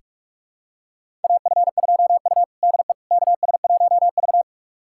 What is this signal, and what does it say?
Morse code, heard as a keyed tone, about 46 wpm, 700 Hz. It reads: AF2U DE KS1V
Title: USA Callsign DE Callsign - 4 Characters - 45wpm